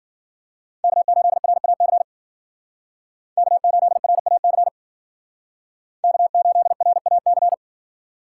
Morse code, heard as a keyed tone, about 52 wpm, 700 Hz. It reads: K8RAC K8RAC K8RAC